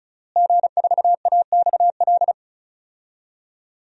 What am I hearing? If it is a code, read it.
Morse code, heard as a keyed tone, about 34 wpm, 700 Hz. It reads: G4AXL
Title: DXCC-100 Call Signs - 5 Characters - Repeated 3x - 35wpm